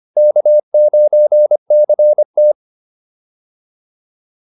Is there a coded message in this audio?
Yes: Morse code from 0.2 to 2.5 s, heard as a keyed tone, about 24 wpm, 600 Hz. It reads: K9CT